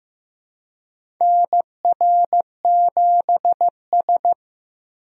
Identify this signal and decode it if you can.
Morse code, heard as a keyed tone, about 15 wpm, 700 Hz. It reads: NR7S
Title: USA Call Signs - 4 Characters - Repeated 3x - 15wpm